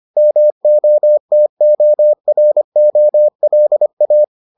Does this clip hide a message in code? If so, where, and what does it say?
Yes: Morse code from 0.2 to 4.2 s, heard as a keyed tone, about 25 wpm, 600 Hz. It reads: MOTOROLA